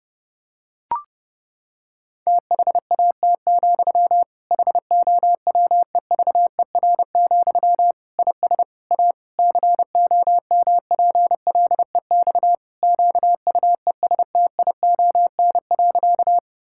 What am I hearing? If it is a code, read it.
Morse code, heard as a keyed tone, about 30 wpm, 700 Hz. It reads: THAT, HOWEVER, IS A COMPLEX QUESTION.
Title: Sentences from Top 1000 Words - Review Part 6 - 30wpm